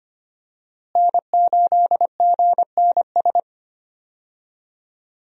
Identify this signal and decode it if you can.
Morse code, heard as a keyed tone, about 25 wpm, 700 Hz. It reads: N8GNS
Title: USA Call Signs - 5 Characters - Repeated 2x - 25wpm